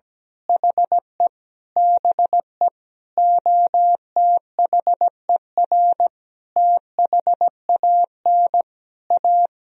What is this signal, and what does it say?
Morse code, heard as a keyed tone, about 17 wpm, 700 Hz. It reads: HE BE OTHER THAN A
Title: Sentences from Top 1000 Words - Review Part 2 - 17wpm